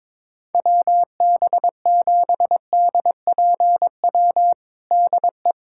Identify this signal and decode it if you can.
Morse code, heard as a keyed tone, about 22 wpm, 700 Hz. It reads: WB7DPW DE
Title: USA Callsign DE Callsign - 6 Characters - 22wpm